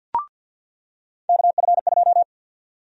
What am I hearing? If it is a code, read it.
Morse code, heard as a keyed tone, about 49 wpm, 700 Hz. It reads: KF2
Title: USA Call Sign Prefixes - 50wpm